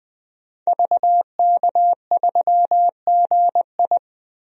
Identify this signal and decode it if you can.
Morse code, heard as a keyed tone, about 20 wpm, 700 Hz. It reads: VK3GI
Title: DXCC-100 Call Signs - 5 Characters - 20wpm